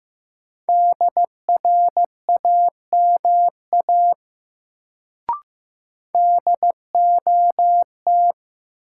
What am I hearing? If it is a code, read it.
Morse code, heard as a keyed tone, about 15 wpm, 700 Hz. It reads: DRAMA DOT
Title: Sentences from Top 1500 Words - New Word Drill - 15wpm